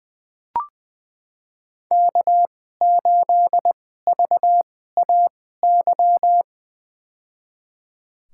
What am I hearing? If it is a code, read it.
Morse code, heard as a keyed tone, about 21 wpm, spaced as for 15 wpm, 700 Hz. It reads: K8VAY
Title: USA Call Signs - 5 Characters - 2x Character Spacing - 20wpm